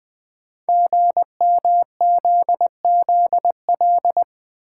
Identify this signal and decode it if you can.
Morse code, heard as a keyed tone, about 20 wpm, 700 Hz. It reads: GMZZL
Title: Code Groups - 5 Letters - 20wpm